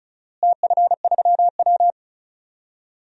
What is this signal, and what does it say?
Morse code, heard as a keyed tone, about 34 wpm, 700 Hz. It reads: TF3W
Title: DXCC-100 Call Signs - 4 Characters - Repeated 3x - 35wpm